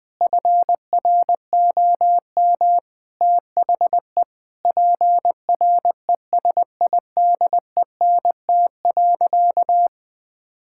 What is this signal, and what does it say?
Morse code, heard as a keyed tone, about 20 wpm, 700 Hz. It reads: FROM THE PRESIDENT.
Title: Sentences from Top 1600 Words 20wpm